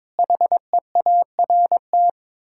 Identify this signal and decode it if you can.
Morse code, heard as a keyed tone, about 22 wpm, 700 Hz. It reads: HEART
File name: practice / 22wpm / Sets of 7 Words - 22wpm